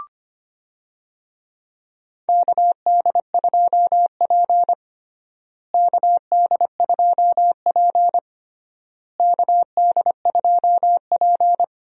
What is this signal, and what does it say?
Morse code, heard as a keyed tone, about 25 wpm, 700 Hz. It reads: KD2P KD2P KD2P
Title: USA Call Signs - 4 Characters - Repeated 3x - 25wpm